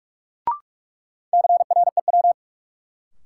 Morse code, heard as a keyed tone, about 41 wpm, 700 Hz. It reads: CREW
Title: Sentences from Top 1500 Words - New Word Drill Alt - 45wpm